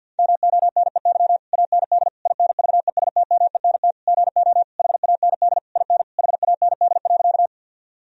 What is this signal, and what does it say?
Morse code, heard as a keyed tone, about 50 wpm, 700 Hz. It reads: MONEY AND INVESTMENT GO HAND IN HAND.
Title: Sentences from Top 1000 Words - Review Part 4 - 50wpm